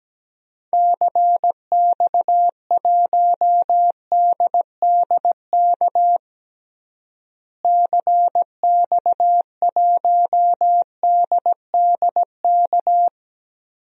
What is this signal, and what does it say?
Morse code, heard as a keyed tone, about 17 wpm, 700 Hz. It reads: CX1DDK CX1DDK
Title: DXCC-100 Call Signs - 6 Characters - Repeated 3x - 17wpm